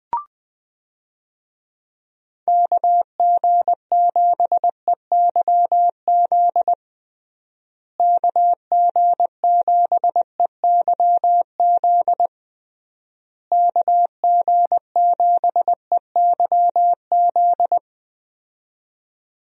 Morse code, heard as a keyed tone, about 21 wpm, 700 Hz. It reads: KG7EYZ KG7EYZ KG7EYZ
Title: USA Call Signs - 6 Characters - Repeated 3x - 20wpm